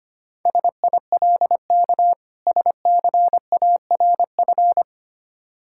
Morse code, heard as a keyed tone, about 25 wpm, 690 Hz. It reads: SILK SCARF
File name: practice / 25wpm / Difficult Words with Hints - 25wpm